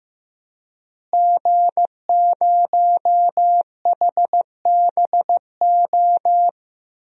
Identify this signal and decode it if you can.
Morse code, heard as a keyed tone, about 15 wpm, 700 Hz. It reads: G0HBO
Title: DXCC-100 Call Signs - 5 Characters - Repeated 3x - 15wpm